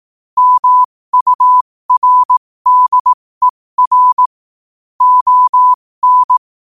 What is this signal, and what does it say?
Morse code, heard as a keyed tone, about 18 wpm, spaced as for 15 wpm, 1000 Hz. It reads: MURDER ON
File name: 15wpm